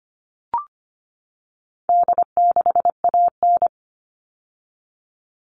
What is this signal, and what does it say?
Morse code, heard as a keyed tone, about 25 wpm, 700 Hz. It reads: D6AN
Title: All Cut Numbers - 4-Digit 25wpm